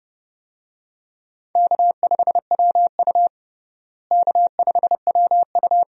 Morse code, heard as a keyed tone, about 30 wpm, 700 Hz. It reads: K5WU K5WU
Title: USA Call Signs - 4 Characters - Repeated 3x - 30wpm